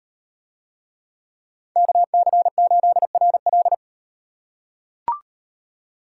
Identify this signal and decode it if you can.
Morse code, heard as a keyed tone, about 38 wpm, 700 Hz. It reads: KC8RL